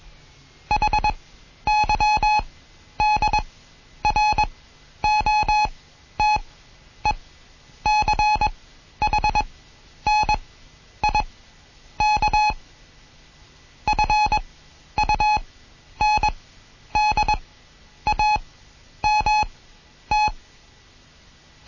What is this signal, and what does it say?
Morse code, heard as a keyed tone, about 21 wpm, spaced as for 10 wpm, 850 Hz. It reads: HYDROTECHNIK FUNDAMT
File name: Morsen / Kurs A 20